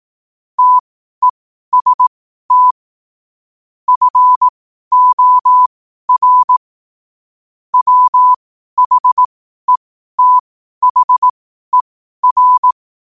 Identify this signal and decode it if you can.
Morse code, heard as a keyed tone, about 18 wpm, spaced as for 12 wpm, 1000 Hz. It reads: TEST FOR WHETHER
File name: qotdfiles / 12wpm